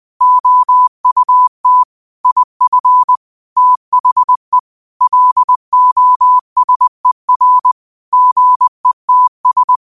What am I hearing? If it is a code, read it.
Morse code, heard as a keyed tone, about 20 wpm, 1000 Hz. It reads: OUT IF THE LOSER GETS